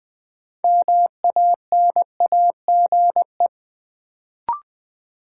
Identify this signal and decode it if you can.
Morse code, heard as a keyed tone, about 20 wpm, 700 Hz. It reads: MANAGE
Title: Sentences from Top 1400 Words - New Word Drill - 20wpm